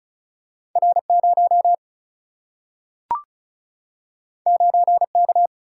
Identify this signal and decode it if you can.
Morse code, heard as a keyed tone, about 36 wpm, 700 Hz. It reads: R0 9K
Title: Single-Letter Single-Number 35wpm